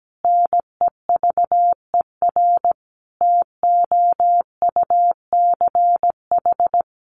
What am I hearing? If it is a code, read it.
Morse code, heard as a keyed tone, about 17 wpm, 700 Hz. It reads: NEVER TOUCH
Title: Sentences from Top 1200 Words 17wpm